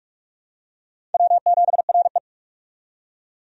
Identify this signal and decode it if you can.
Morse code, heard as a keyed tone, about 46 wpm, 700 Hz. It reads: W7RE
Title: USA Call Signs - 4 Characters - 45wpm